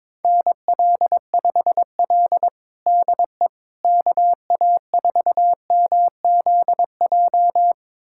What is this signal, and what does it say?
Morse code, heard as a keyed tone, about 22 wpm, 700 Hz. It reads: NL5L DE KA4MZJ
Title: USA Callsign DE Callsign 22wpm